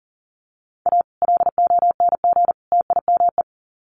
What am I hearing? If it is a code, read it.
Morse code, heard as a keyed tone, about 39 wpm, 700 Hz. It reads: A LONG TIME
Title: Sets of 3 Words 40wpm